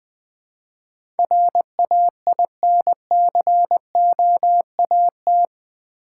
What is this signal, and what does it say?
Morse code, heard as a keyed tone, about 20 wpm, 700 Hz. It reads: RAINCOAT
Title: Easy Words with Hints - 20wpm